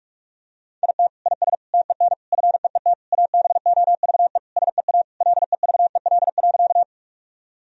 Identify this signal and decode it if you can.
Morse code, heard as a keyed tone, about 45 wpm, 700 Hz. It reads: IT IS TEN FEET ABOVE SEA LEVEL.
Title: Sentences from Top 1000 Words - Review Part 4 - 45wpm